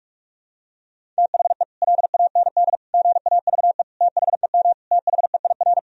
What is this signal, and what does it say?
Morse code, heard as a keyed tone, about 41 wpm, 700 Hz. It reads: THE LAND GAVE THEM THEIR
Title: Sentences from Top 1000 Words - Review Part 1 - 45wpm